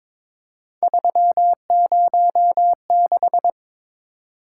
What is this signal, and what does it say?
Morse code, heard as a keyed tone, about 23 wpm, 700 Hz. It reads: 306
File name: practice / 22wpm / Three Letter-Number Combinations - 22wpm